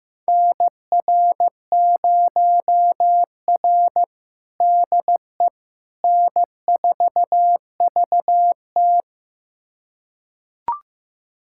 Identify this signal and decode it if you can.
Morse code, heard as a keyed tone, about 15 wpm, 700 Hz. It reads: NR0R DE N4VT
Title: USA Callsign DE Callsign - 4 Characters - 15wpm